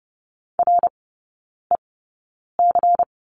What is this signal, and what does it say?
Morse code, heard as a keyed tone, about 30 wpm, spaced as for 9 wpm, 700 Hz. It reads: REC
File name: Lesson 16e - Letter c - Three Characters 30wpm